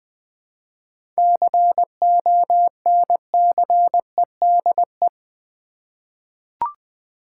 Morse code, heard as a keyed tone, about 20 wpm, 700 Hz. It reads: CONCEDE
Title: 7 Letter Words - First 3 Letters Given - 20wpm